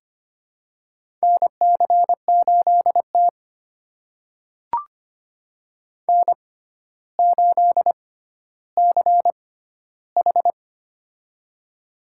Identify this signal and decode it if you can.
Morse code, heard as a keyed tone, about 25 wpm, spaced as for 13 wpm, 700 Hz. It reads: NC8T N8CH
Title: USA Call Signs - 4 Characters - 6x Character Spacing - 25wpm